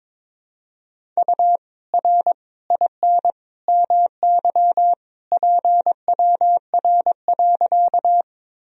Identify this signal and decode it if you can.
Morse code, heard as a keyed tone, about 21 wpm, 700 Hz. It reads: U R IN MY PWR.
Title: Sentences from Top 500 Words and Abbreviations 22wpm